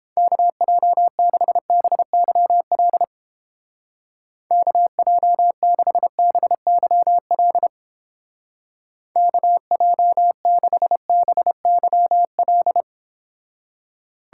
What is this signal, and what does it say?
Morse code, heard as a keyed tone, about 30 wpm, 700 Hz. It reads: KJ6BYL KJ6BYL KJ6BYL